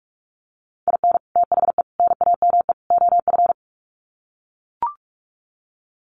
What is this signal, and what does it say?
Morse code, heard as a keyed tone, about 44 wpm, 700 Hz. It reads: IN THE NAME OF
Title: Sets of 4 Words 45wpm